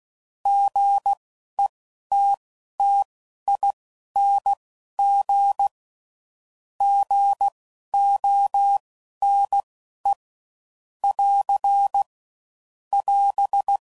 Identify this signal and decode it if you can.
Morse code, heard as a keyed tone, about 16 wpm, spaced as for 12 wpm, 800 Hz. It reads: GETTING GONE <AR> <AS>